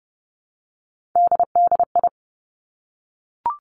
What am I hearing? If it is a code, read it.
Morse code, heard as a keyed tone, about 30 wpm, 700 Hz. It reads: DDI
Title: Three Challenging Letter-Number Combinations 30wpm